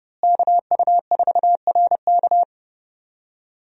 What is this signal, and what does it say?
Morse code, heard as a keyed tone, about 30 wpm, 700 Hz. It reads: KU4RK
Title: USA Call Signs - 5 Characters - Repeated 2x - 30wpm